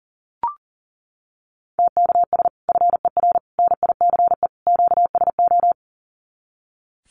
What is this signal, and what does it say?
Morse code, heard as a keyed tone, about 40 wpm, 700 Hz. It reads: TKS FER NICE QSO
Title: Common QSO Phrases 40wpm